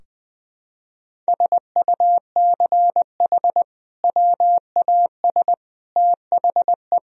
Morse code, heard as a keyed tone, about 21 wpm, 700 Hz. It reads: SUCH WAS THE
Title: Sentences from Top 1000 Words - Review Part 3 - 20wpm